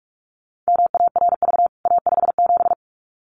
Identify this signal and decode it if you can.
Morse code, heard as a keyed tone, about 44 wpm, 700 Hz. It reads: MARV A57
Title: CWT Exchange 45wpm